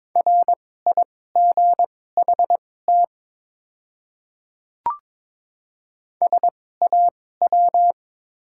Morse code, heard as a keyed tone, about 21 wpm, spaced as for 16 wpm, 700 Hz. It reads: RIGHT SAW
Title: 2x Character Spacing - Top 300 Words - 22wpm